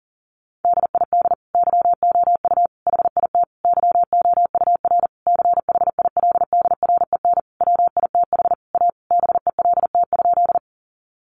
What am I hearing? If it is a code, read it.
Morse code, heard as a keyed tone, about 40 wpm, 700 Hz. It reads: DID YOU HIT YOUR CHILDREN WITH A BELT?
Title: Sentences from Top 1400 Words 40wpm